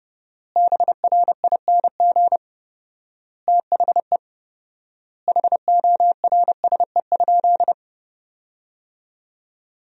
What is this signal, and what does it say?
Morse code, heard as a keyed tone, about 29 wpm, 700 Hz. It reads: BRING THE HORSE?
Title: 4x Word-Spacing - Sentences from Top 500 Words - 30wpm